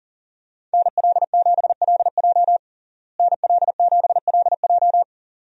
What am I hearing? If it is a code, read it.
Morse code, heard as a keyed tone, about 40 wpm, 700 Hz. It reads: NL7LJ NL7LJ
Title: USA Call Signs - 5 Characters - Repeated 2x - 40wpm